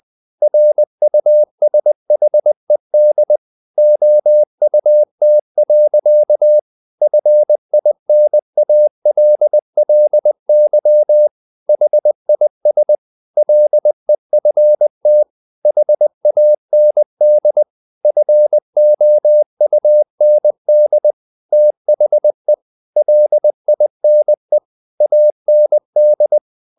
Morse code, heard as a keyed tone, about 20 wpm, 600 Hz. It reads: RUSHED OUT. FINALLY HIS LEFT HAND FOUND THE LINE AND
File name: the old man and the sea part 2 20wpm